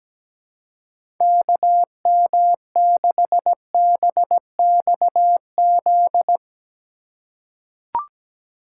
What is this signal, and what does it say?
Morse code, heard as a keyed tone, about 17 wpm, 700 Hz. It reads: KM6BXZ
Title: USA Call Signs - 6 Characters - 4x Character Spacing - 17wpm